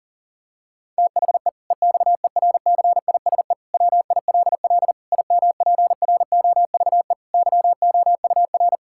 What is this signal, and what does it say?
Morse code, heard as a keyed tone, about 40 wpm, 700 Hz. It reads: THE EXERCISE WILL IMPROVE YOUR
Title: Sentences from Top 1000 Words - Review Part 6 - 40wpm